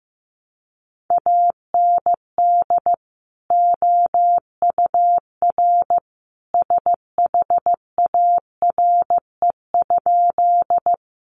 Morse code, heard as a keyed tone, about 15 wpm, 700 Hz. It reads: AND OUR SHARE?
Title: Sentences from Top 900 Words 15wpm